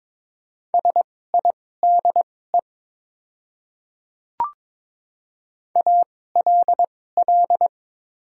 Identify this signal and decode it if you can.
Morse code, heard as a keyed tone, about 22 wpm, spaced as for 16 wpm, 690 Hz. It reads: SIDE ALL
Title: 2x Character Spacing - Top 100 Words - 22wpm